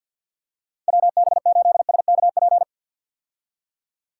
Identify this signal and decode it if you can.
Morse code, heard as a keyed tone, about 51 wpm, 700 Hz. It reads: WB8SKP
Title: K1USN SST QSO Practice - 50wpm